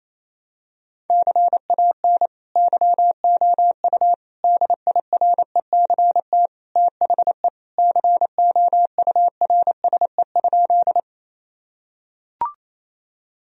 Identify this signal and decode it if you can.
Morse code, heard as a keyed tone, about 28 wpm, 700 Hz. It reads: CAN YOU DIRECT THE COURSE?
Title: Sentences from Top 1000 Words - Review Part 2 - 28wpm